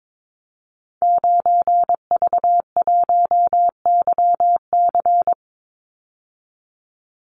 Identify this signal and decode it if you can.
Morse code, heard as a keyed tone, about 22 wpm, 700 Hz. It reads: 9V1YC
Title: Call Signs - 22wpm